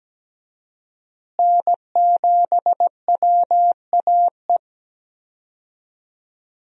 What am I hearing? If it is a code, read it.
Morse code, heard as a keyed tone, about 17 wpm, 700 Hz. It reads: N7WAE
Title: USA Call Signs - 5 Characters - Repeated 2x - 17wpm